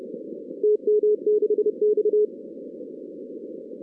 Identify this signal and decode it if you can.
Morse code, heard as a keyed tone, about 30 wpm, 410 Hz. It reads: TM6X
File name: 2016 arrl dx cw / 2016-ARRL-DX-CW-K5TR- - 0-46